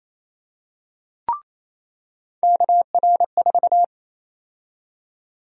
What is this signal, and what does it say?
Morse code, heard as a keyed tone, about 27 wpm, 700 Hz. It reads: KR4
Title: USA Call Sign Prefixes - 28wpm